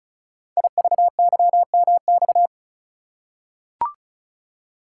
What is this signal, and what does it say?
Morse code, heard as a keyed tone, about 36 wpm, 690 Hz. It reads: IVYMX
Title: Code Groups - 5 Letters - 35wpm